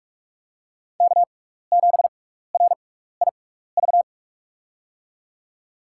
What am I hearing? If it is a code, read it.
Morse code, heard as a keyed tone, about 48 wpm, spaced as for 16 wpm, 700 Hz. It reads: K7RIV
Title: USA Call Signs - 5 Characters - 6x Character Spacing - 45wpm